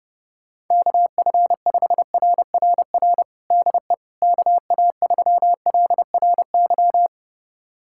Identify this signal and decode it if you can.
Morse code, heard as a keyed tone, about 30 wpm, 700 Hz. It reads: KF5RRR DE KA3LRY
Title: USA Callsign DE Callsign - 6 Characters - 30wpm